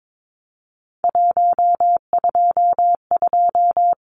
Morse code, heard as a keyed tone, about 22 wpm, 700 Hz. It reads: 122
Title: Numbers - 3-Digit 22wpm